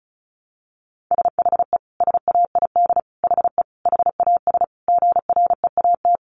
Morse code, heard as a keyed tone, about 34 wpm, 700 Hz. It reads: SHE SAID HE HAS GREAT
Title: Sentences from Top 1300 Words 35wpm